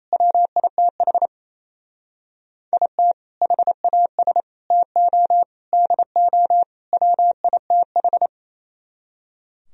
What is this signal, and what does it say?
Morse code, heard as a keyed tone, about 35 wpm, 700 Hz. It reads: WITH IT HAS TO DO WITH